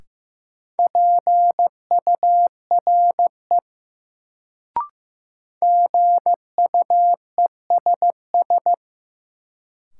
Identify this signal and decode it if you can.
Morse code, heard as a keyed tone, about 15 wpm, 700 Hz. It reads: PURE GUESS
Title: Sentences from Top 1500 Words - New Word Drill Alt - 15wpm